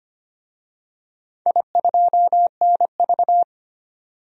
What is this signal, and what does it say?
Morse code, heard as a keyed tone, about 25 wpm, 700 Hz. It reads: I2NV
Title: DXCC-100 Call Signs - 4 Characters - 25wpm